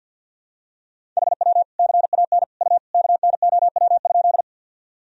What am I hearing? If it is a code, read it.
Morse code, heard as a keyed tone, about 46 wpm, 700 Hz. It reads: HW CAN U KNOW?
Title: Sentences from Top 100 Words and Abbreviations 50wpm